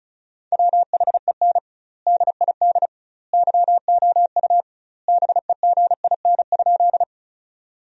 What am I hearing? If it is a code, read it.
Morse code, heard as a keyed tone, about 34 wpm, 700 Hz. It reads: WHEN DID YOU BEGIN?
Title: 2x Word-Spacing - Sentences from Top 300 Words - 35wpm